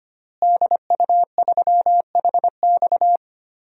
Morse code, heard as a keyed tone, about 25 wpm, 700 Hz. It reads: DU3HX
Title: DXCC-100 Call Signs - 5 Characters - Repeated 2x - 25wpm